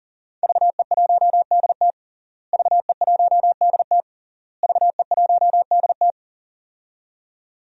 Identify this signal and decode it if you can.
Morse code, heard as a keyed tone, about 40 wpm, 700 Hz. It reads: VE1DT VE1DT VE1DT
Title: DXCC-100 Call Signs - 5 Characters - Repeated 3x - 40wpm